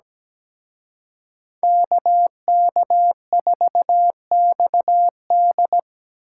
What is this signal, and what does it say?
Morse code, heard as a keyed tone, about 20 wpm, 700 Hz. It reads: KK4XD
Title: Speed Racing - USA 2x2 Call Signs 17wpm